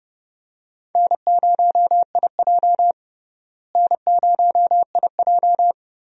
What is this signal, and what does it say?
Morse code, heard as a keyed tone, about 30 wpm, 700 Hz. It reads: N0IJ N0IJ
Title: USA Call Signs - 4 Characters - Repeated 3x - 30wpm